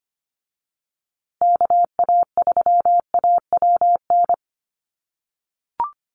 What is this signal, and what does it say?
Morse code, heard as a keyed tone, about 25 wpm, 700 Hz. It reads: KA3AWN
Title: USA Call Signs Repeated 2x 25wpm